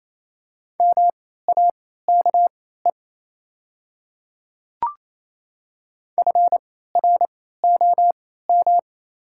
Morse code, heard as a keyed tone, about 28 wpm, spaced as for 16 wpm, 700 Hz. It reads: MAKE FROM
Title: 3x Character Spacing - Top 100 Words - 28wpm